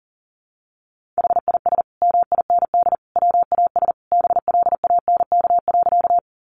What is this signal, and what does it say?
Morse code, heard as a keyed tone, about 40 wpm, 700 Hz. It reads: HIS MIND WAS BLANK.
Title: Sentences from Top 1500 Words 40wpm